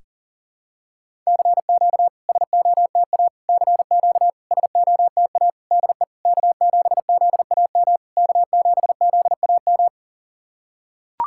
Morse code, heard as a keyed tone, about 40 wpm, 690 Hz. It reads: CQ SOTA CQ SOTA DE K7ZAM K7ZAM
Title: SOTA QSO Practice 40wpm